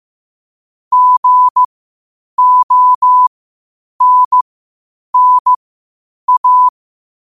Morse code, heard as a keyed tone, about 15 wpm, spaced as for 8 wpm, 1000 Hz. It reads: GONNA
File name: qotdfiles / 8wpm